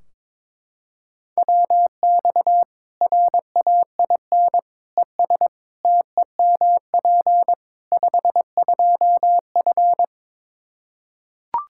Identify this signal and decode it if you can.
Morse code, heard as a keyed tone, about 22 wpm, 700 Hz. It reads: WX RAIN ES TEMP 52F
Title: Practice QSO - Exchange 2 - Info - 22wpm